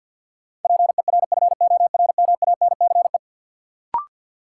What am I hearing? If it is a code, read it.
Morse code, heard as a keyed tone, about 53 wpm, 700 Hz. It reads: PERFORMANCE
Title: Sentences from Top 1000 Words - New Word Drill - 50wpm